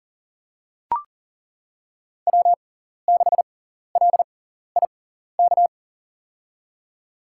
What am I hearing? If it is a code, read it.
Morse code, heard as a keyed tone, about 40 wpm, spaced as for 14 wpm, 700 Hz. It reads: W6LIK
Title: USA Call Signs - 5 Characters - 6x Character Spacing - 40wpm